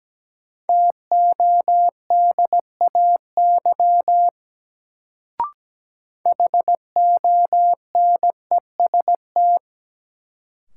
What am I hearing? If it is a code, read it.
Morse code, heard as a keyed tone, about 17 wpm, 700 Hz. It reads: TODAY HONEST
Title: Sets of 5 Words Encore - New Word Drill - 17wpm